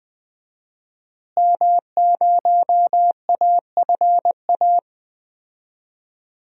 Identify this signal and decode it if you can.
Morse code, heard as a keyed tone, about 20 wpm, 700 Hz. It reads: M0AFA
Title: DXCC-100 Call Signs - 5 Characters - Repeated 2x - 20wpm